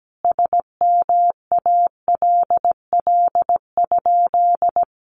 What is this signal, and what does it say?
Morse code, heard as a keyed tone, about 17 wpm, 700 Hz. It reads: SMALL?
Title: Sentences from Top 200 Words 17wpm